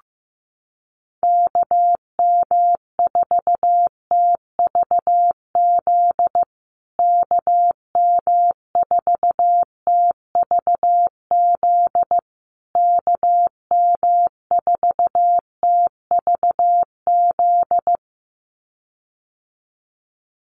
Morse code, heard as a keyed tone, about 15 wpm, 700 Hz. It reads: KM4TVZ KM4TVZ KM4TVZ
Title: USA Call Signs Repeated 3x 15wpm